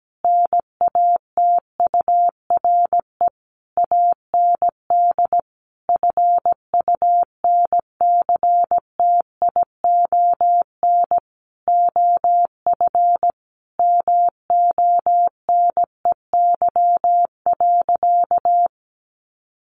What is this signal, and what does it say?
Morse code, heard as a keyed tone, about 17 wpm, 700 Hz. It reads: NATURE AND FUNCTION OF MONEY.
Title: Sentences from Top 1100 Words 17wpm